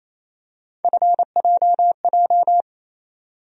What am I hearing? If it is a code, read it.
Morse code, heard as a keyed tone, about 28 wpm, 700 Hz. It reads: FJJ
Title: USA Call Sign Postfixes - 3 Letters 28wpm